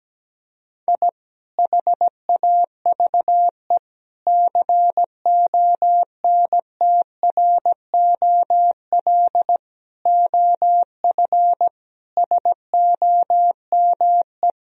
Morse code, heard as a keyed tone, about 17 wpm, 700 Hz. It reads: I HAVE CONTROL OF SOME
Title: Sentences from Top 1000 Words - Review Part 3 - 17wpm